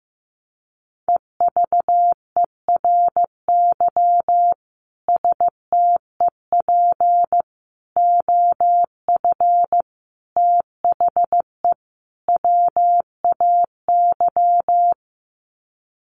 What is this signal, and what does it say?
Morse code, heard as a keyed tone, about 15 wpm, 700 Hz. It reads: EVERY STEP OF THE WAY
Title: Sets of 5 Words Encore2 15wpm